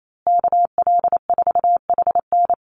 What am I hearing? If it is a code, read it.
Morse code, heard as a keyed tone, about 28 wpm, 700 Hz. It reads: KL4HN
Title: USA Call Signs Repeated 3x 28wpm